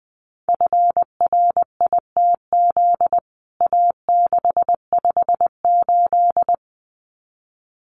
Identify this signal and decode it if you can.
Morse code, heard as a keyed tone, about 19 wpm, 700 Hz. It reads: FRITZ A658
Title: CWT Exchange 20wpm